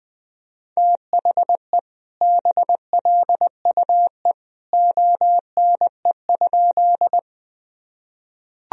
Morse code, heard as a keyed tone, about 20 wpm, 700 Hz. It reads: THE BLUE ONE?
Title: Sentences from Top 1000 Words - Review Part 3 - 20wpm